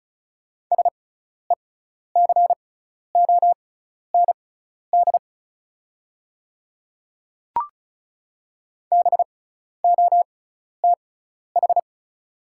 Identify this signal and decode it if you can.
Morse code, heard as a keyed tone, about 36 wpm, spaced as for 12 wpm, 690 Hz. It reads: SECOND BOTH
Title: 6x Character Spacing - Top 300 Words - 35wpm